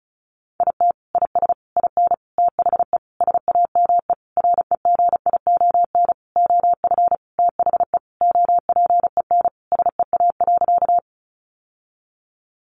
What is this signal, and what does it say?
Morse code, heard as a keyed tone, about 34 wpm, 700 Hz. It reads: IT IS IN THE SAME REGION OF THE OPEN SEA.
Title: Sentences from Top 1500 Words 35wpm